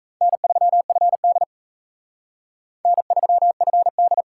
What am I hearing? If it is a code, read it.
Morse code, heard as a keyed tone, about 34 wpm, 700 Hz. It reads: N3FD N3FD
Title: Speed Racing - USA 1x2 Call Signs 28wpm